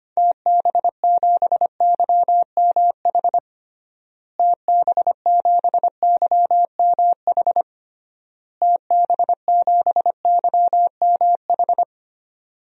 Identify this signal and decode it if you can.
Morse code, heard as a keyed tone, about 25 wpm, 690 Hz. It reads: TB7YMH TB7YMH TB7YMH
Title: DXCC-100 Call Signs - 6 Characters - Repeated 3x - 25wpm